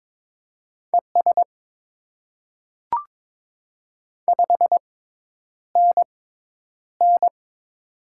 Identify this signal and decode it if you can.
Morse code, heard as a keyed tone, about 22 wpm, spaced as for 10 wpm, 700 Hz. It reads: ES 5NN
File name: Top 20 QSO Elements - 6x Character Spacing - 22wpm